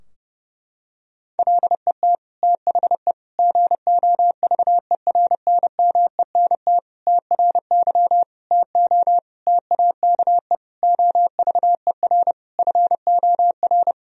Morse code, heard as a keyed tone, about 31 wpm, 700 Hz. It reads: LET THE GOVERNMENT TRY TO TAKE OVER FOR
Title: Sentences from Top 1000 Words - Review Part 3 - 30wpm